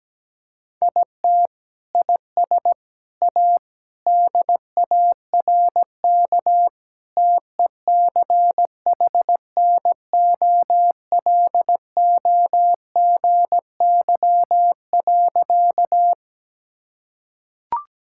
Morse code, heard as a keyed tone, about 17 wpm, 700 Hz. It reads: IT IS A DARK TECHNOLOGY.
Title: Sentences from Top 1000 Words - Review Part 5 - 17wpm